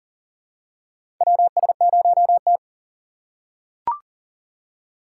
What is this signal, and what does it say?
Morse code, heard as a keyed tone, about 37 wpm, 700 Hz. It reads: WS0T
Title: USA Call Signs - 4 Characters - 4x Character Spacing - 40wpm